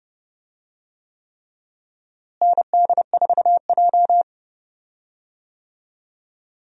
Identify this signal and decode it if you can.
Morse code, heard as a keyed tone, about 30 wpm, 700 Hz. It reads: ND4J
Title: USA Call Signs - 4 Characters - Repeated 3x - 30wpm